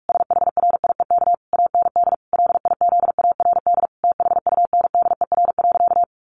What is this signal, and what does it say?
Morse code, heard as a keyed tone, about 44 wpm, 700 Hz. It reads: SHRIEK AND LIZARD THUNDER.
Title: A Sound Of Thunder 45wpm